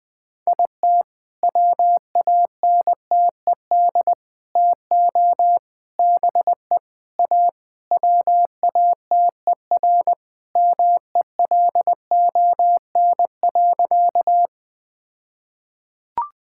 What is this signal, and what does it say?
Morse code, heard as a keyed tone, about 20 wpm, 700 Hz. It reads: IT WANTED TO BE A WATER MELON.
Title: All-Age Appropriate Jokes - 20wpm